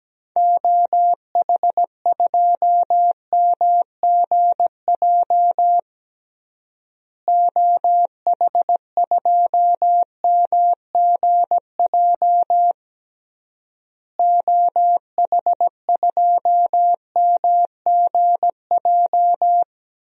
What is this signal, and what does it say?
Morse code, heard as a keyed tone, about 17 wpm, 700 Hz. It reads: OH2MGJ OH2MGJ OH2MGJ
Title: DXCC-100 Call Signs - 6 Characters - Repeated 3x - 17wpm